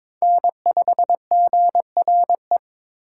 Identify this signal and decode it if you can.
Morse code, heard as a keyed tone, about 26 wpm, 700 Hz. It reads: N5GRE
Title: Speed Racing - USA Call Signs 22wpm